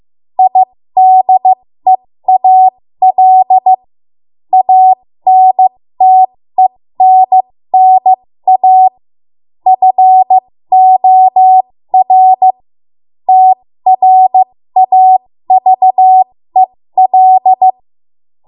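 Morse code, heard as a keyed tone, about 15 wpm, spaced as for 13 wpm, 750 Hz. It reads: IDEAL ANTENNA FOR TRAVEL
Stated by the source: ARRL Morse code practice at 13 WPM.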